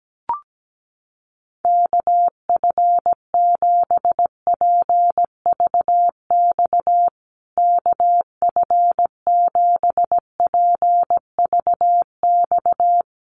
Morse code, heard as a keyed tone, about 17 wpm, 700 Hz. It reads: KF7PVX KF7PVX
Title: USA Call Signs Repeated 3x 17wpm